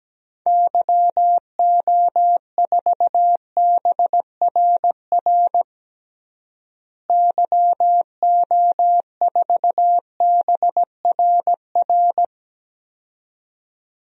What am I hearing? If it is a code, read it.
Morse code, heard as a keyed tone, about 17 wpm, 700 Hz. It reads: YO4BRR YO4BRR
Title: DXCC-100 Call Signs - 6 Characters - Repeated 2x - 17wpm